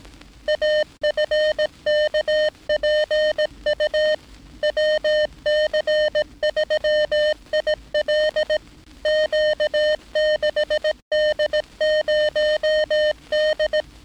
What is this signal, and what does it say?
Morse code, heard as a keyed tone, about 18 wpm, 610 Hz. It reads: AFKPU WC3IL Q6D0D
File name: training / NAVPERS 11049to11060 / NAVPERS 11059 RA RB